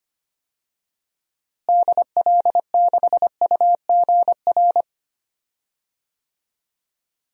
Morse code, heard as a keyed tone, about 25 wpm, 690 Hz. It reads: DL6UGR
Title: DXCC-100 Call Signs - 6 Characters - Repeated 3x - 25wpm